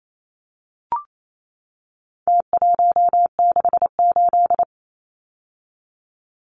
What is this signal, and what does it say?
Morse code, heard as a keyed tone, about 28 wpm, 700 Hz. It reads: T168
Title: Numbers - Contest Style - 4-Digit 28wpm